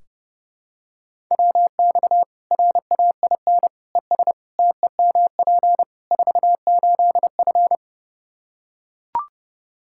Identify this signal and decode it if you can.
Morse code, heard as a keyed tone, about 30 wpm, 700 Hz. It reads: WX RAIN ES TEMP 48F
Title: Practice QSO - Exchange 2 - Info - 30wpm